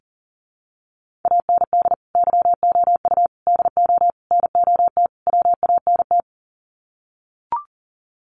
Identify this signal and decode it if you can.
Morse code, heard as a keyed tone, about 39 wpm, 700 Hz. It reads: AND YOU DO NOT WANT
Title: Sets of 5 Words Encore2 40wpm